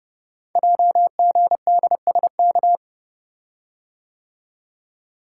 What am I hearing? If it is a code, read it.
Morse code, heard as a keyed tone, about 30 wpm, 700 Hz. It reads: JGDSK
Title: Code Groups - 5 Letter:Numbers - 30wpm